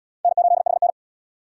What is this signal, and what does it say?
Morse code, heard as a keyed tone, about 66 wpm, 700 Hz. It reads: N8HN